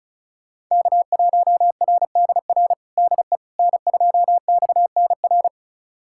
Morse code, heard as a keyed tone, about 34 wpm, 700 Hz. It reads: K1RDR DE N2XNR
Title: USA Callsign DE Callsign - 5 Characters - 35wpm